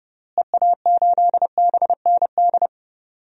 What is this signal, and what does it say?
Morse code, heard as a keyed tone, about 30 wpm, 700 Hz. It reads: EA8BND
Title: DXCC-100 Call Signs - 6 Characters - Repeated 3x - 30wpm